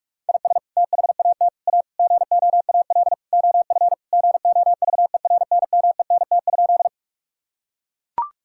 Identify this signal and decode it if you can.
Morse code, heard as a keyed tone, about 42 wpm, 700 Hz. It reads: IS THAT A GOAL OF GOVERNMENT?
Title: Sentences from Top 1000 Words - Review Part 5 - 45wpm